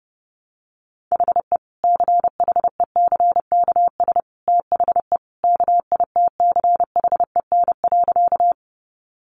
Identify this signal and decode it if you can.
Morse code, heard as a keyed tone, about 30 wpm, 700 Hz. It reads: HE CHECKS THE KITCHEN.
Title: Easy Phrases 30wpm